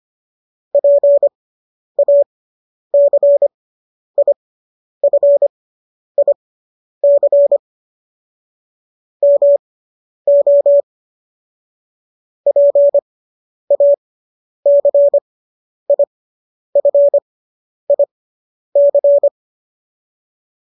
Morse code, heard as a keyed tone, about 25 wpm, spaced as for 10 wpm, 570 Hz. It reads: PACIFIC MO PACIFIC